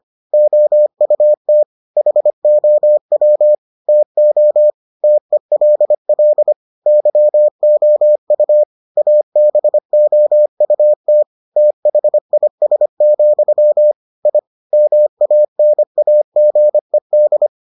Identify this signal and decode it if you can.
Morse code, heard as a keyed tone, about 25 wpm, 600 Hz. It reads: OUT HOW TO TELL YOU ABOUT THIS, I MANAGED